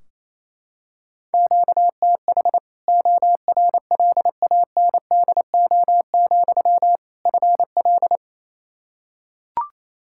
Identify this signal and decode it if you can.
Morse code, heard as a keyed tone, about 28 wpm, 700 Hz. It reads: QTH ORLANDO, FL
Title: Practice QSO - Exchange 1 - Beginning - 28wpm